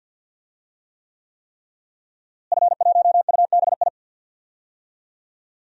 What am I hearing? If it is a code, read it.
Morse code, heard as a keyed tone, about 50 wpm, 700 Hz. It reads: F1UBI
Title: DXCC-100 Call Signs - 5 Characters - Repeated 3x - 50wpm